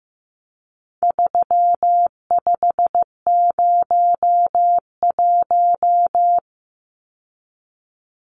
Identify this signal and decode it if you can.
Morse code, heard as a keyed tone, about 15 wpm, 700 Hz. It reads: 3501
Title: Numbers - 4-Digit 15wpm